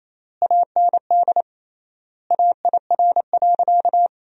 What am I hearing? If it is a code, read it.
Morse code, heard as a keyed tone, about 28 wpm, 700 Hz. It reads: AND AIR.
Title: 3x Word-Spacing - Sentences from Top 400 Words - 28wpm